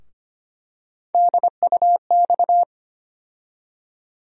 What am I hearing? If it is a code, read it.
Morse code, heard as a keyed tone, about 25 wpm, 700 Hz. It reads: DUX
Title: USA Call Sign Postfixes - 3 Letters 25wpm